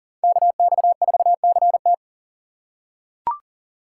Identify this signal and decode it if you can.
Morse code, heard as a keyed tone, about 38 wpm, 700 Hz. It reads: KX4CT
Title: USA Call Signs - 5 Characters - 2x Character Spacing - 40wpm